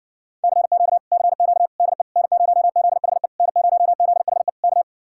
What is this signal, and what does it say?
Morse code, heard as a keyed tone, about 60 wpm, 700 Hz. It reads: CQ CQ DE N0ZHE N0ZHE K